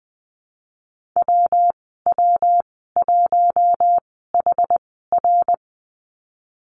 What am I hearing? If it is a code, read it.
Morse code, heard as a keyed tone, about 20 wpm, spaced as for 15 wpm, 700 Hz. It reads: WW1HR
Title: 173 Lesson 28g - Number 9 - Call Signs 20wpm